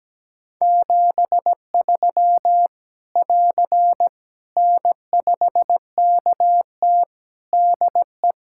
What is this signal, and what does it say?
Morse code, heard as a keyed tone, about 17 wpm, 700 Hz. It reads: 73 <AR> N5KT DE
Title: Practice QSO - Exchange 4 - Ending - 17wpm